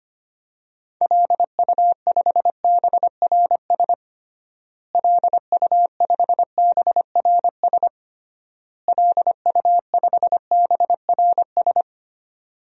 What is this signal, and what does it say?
Morse code, heard as a keyed tone, about 25 wpm, 700 Hz. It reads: LU5BRS LU5BRS LU5BRS
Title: DXCC-100 Call Signs - 6 Characters - Repeated 3x - 25wpm